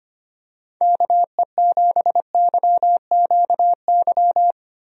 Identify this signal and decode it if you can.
Morse code, heard as a keyed tone, about 25 wpm, 700 Hz. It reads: KE7YQY
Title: USA Call Signs - 6 Characters - Repeated 3x - 25wpm